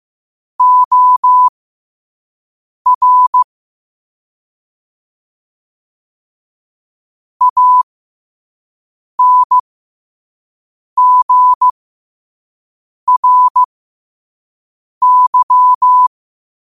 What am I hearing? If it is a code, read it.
Morse code, heard as a keyed tone, about 15 wpm, spaced as for 5 wpm, 1000 Hz. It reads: OR ANGRY
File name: qotdfiles / 5wpm